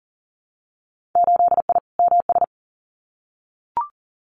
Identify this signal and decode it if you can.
Morse code, heard as a keyed tone, about 40 wpm, 700 Hz. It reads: 8I MS
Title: Winter Field Day Exchanges Encore 40wpm